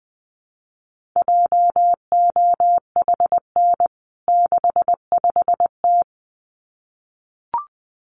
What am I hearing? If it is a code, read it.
Morse code, heard as a keyed tone, about 20 wpm, 700 Hz. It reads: JOHN 65T
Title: CWT Exchange 20wpm